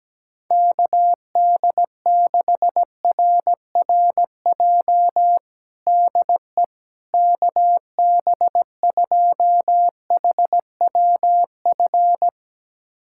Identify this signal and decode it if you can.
Morse code, heard as a keyed tone, about 17 wpm, 700 Hz. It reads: KD6RRJ DE KB2HWF
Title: USA Callsign DE Callsign - 6 Characters - 17wpm